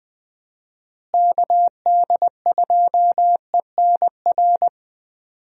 Morse code, heard as a keyed tone, about 21 wpm, 700 Hz. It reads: KD2ENR
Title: USA Call Signs - 6 Characters - Repeated 2x - 20wpm